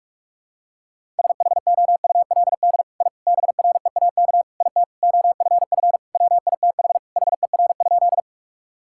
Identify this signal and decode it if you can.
Morse code, heard as a keyed tone, about 46 wpm, 700 Hz. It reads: SHOULD I BREAK IT OFF WITH HER?
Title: Sentences from Top 1000 Words - Review Part 6 - 45wpm